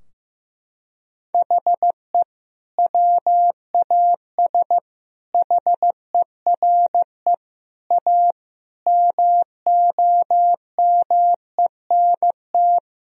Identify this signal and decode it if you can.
Morse code, heard as a keyed tone, about 16 wpm, 700 Hz. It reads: HE WAS HERE A MOMENT
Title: Sentences from Top 1000 Words - Review Part 3 - 15wpm